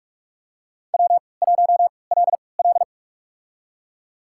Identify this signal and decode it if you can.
Morse code, heard as a keyed tone, about 48 wpm, spaced as for 26 wpm, 700 Hz. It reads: W1LL
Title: USA Call Signs - 4 Characters - 3x Character Spacing - 45wpm